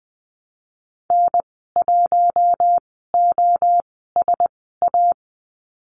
Morse code, heard as a keyed tone, about 20 wpm, spaced as for 14 wpm, 700 Hz. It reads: N1OSA
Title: 042 Lesson 8g - Number 1 - Call Signs 20wpm